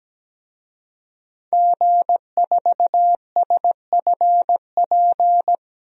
Morse code, heard as a keyed tone, about 17 wpm, 700 Hz. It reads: G4SFP
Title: DXCC-100 Call Signs - 5 Characters - Repeated 2x - 17wpm